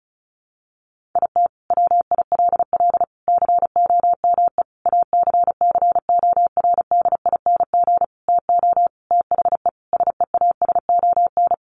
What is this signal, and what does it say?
Morse code, heard as a keyed tone, about 34 wpm, 700 Hz. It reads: IT WILL COME ACCORDING TO THE SEASON
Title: Sentences from Top 1400 Words 35wpm